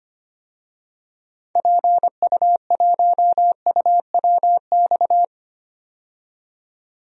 Morse code, heard as a keyed tone, about 25 wpm, 700 Hz. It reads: PU1UWX
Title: DXCC-100 Call Signs - 6 Characters - 25wpm